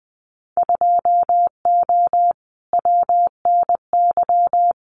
Morse code, heard as a keyed tone, about 20 wpm, 700 Hz. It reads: 2O WNY
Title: Winter Field Day Exchanges Encore 20wpm